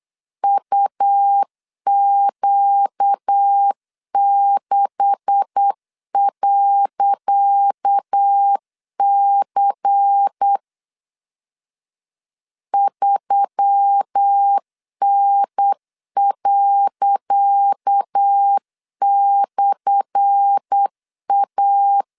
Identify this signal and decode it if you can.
Morse code, heard as a keyed tone, about 8 wpm, 800 Hz. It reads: UQ6.C 3N./A
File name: LEZIONI / 10 SIGNS / 10C DOT 8wpm 3649